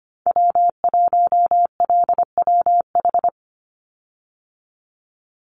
Morse code, heard as a keyed tone, about 25 wpm, 700 Hz. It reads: W1LWH
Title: Call Signs Repeated 2x - 25wpm